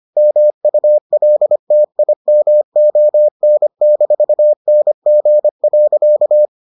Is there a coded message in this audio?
Yes: Morse code from 0.2 to 6.5 s, heard as a keyed tone, about 25 wpm, 600 Hz. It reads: MULTIMON-NG.